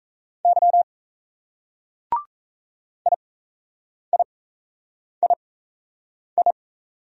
Morse code, heard as a keyed tone, about 38 wpm, spaced as for 8 wpm, 700 Hz. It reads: Y IIII